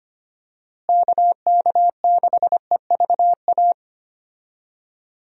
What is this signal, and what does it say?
Morse code, heard as a keyed tone, about 25 wpm, 700 Hz. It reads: KK6EVA
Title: USA Call Signs - 6 Characters - Repeated 2x - 25wpm